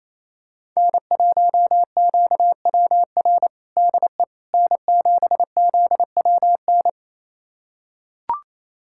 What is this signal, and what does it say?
Morse code, heard as a keyed tone, about 28 wpm, 690 Hz. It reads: N1QWR DE N7ZWN
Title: USA Callsign DE Callsign - 5 Characters - 28wpm